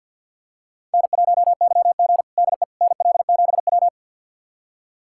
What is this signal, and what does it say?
Morse code, heard as a keyed tone, about 52 wpm, 700 Hz. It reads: N1YG DE NL7W
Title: USA Callsign DE Callsign - 4 Characters - 50wpm